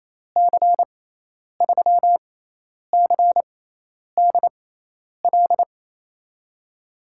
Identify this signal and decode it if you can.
Morse code, heard as a keyed tone, about 28 wpm, spaced as for 10 wpm, 700 Hz. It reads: C3CDL
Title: USA Call Signs - 6 Characters - 6x Character Spacing - 28wpm